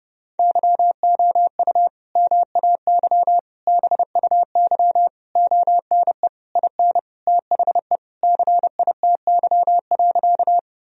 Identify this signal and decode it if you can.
Morse code, heard as a keyed tone, about 30 wpm, 700 Hz. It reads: YOU MAY BUY ONE IN THE CITY.
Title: Sentences from Top 1000 Words - Review Part 4 - 30wpm